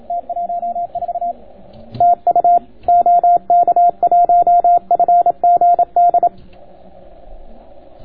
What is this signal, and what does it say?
Morse code, heard as a keyed tone, about 27 wpm, 700 Hz. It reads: T14 TU OK1FGD